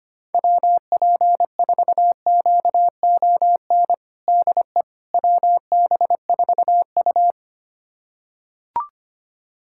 Morse code, heard as a keyed tone, about 25 wpm, 700 Hz. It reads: WP4QON DE WB4U
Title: USA Callsign DE Callsign 25wpm